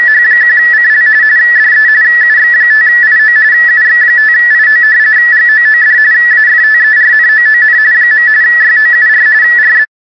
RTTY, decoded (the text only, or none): H1CL DE UX5IO UX5IO GOOD AFTERNOON . YOUR RB